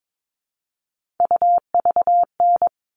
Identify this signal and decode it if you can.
Morse code, heard as a keyed tone, about 22 wpm, 700 Hz. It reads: UVN
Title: All Cut Numbers - 3-Digit 22wpm